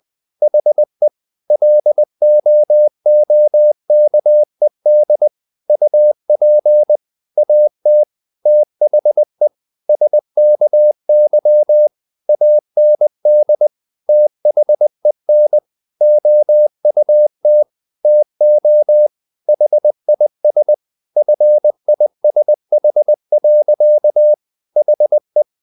Morse code, heard as a keyed tone, about 20 wpm, 600 Hz. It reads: HE LOOKED UP AT THE SKY AND THEN OUT TO HIS FISH. HE
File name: the old man and the sea part 2 20wpm